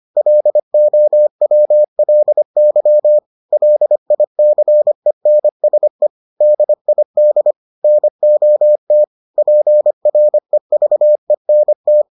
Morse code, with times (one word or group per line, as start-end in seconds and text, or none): 0.2-3.2 LOWLY
3.5-6.1 LICENSE
6.4-7.5 DID
7.8-9.0 NOT
9.4-12.0 PREVENT